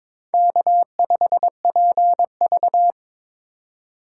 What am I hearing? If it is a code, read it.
Morse code, heard as a keyed tone, about 26 wpm, 700 Hz. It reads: K5PV
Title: Speed Racing - USA 1x2 Call Signs 22wpm